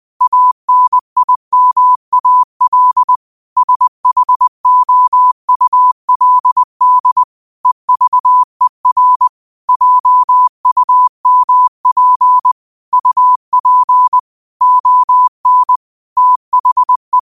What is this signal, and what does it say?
Morse code, heard as a keyed tone, about 20 wpm, 1000 Hz. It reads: ANIMAL SHOULD EVER JUMP UP ON THE
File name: qotdfiles / 20wpm